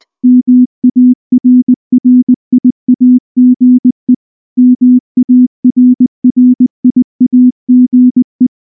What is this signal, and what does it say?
Morse code, heard as a keyed tone, about 19 wpm, 250 Hz. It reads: MARRIAGE MARRIAGE
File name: morse / morse-training-20WPM-1